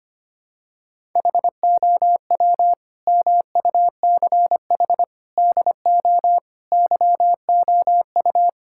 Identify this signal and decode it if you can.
Morse code, heard as a keyed tone, about 25 wpm, 700 Hz. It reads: HOW MUCH DO YOU
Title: Sentences from Top 1000 Words - Review Part 6 - 25wpm